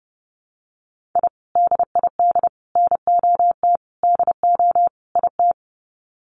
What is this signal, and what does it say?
Morse code, heard as a keyed tone, about 30 wpm, 700 Hz. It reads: I DID NOT DO IT
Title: Sets of 5 Words Encore2 30wpm